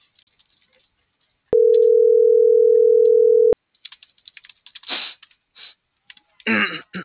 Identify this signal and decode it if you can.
Ringback tone (North America), 440 + 480 Hz.